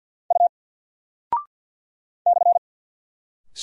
Morse code, heard as a keyed tone, about 48 wpm, 700 Hz. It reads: U /
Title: Single Character (Letter:Number:Punctuation:Prosign) Alt 50wpm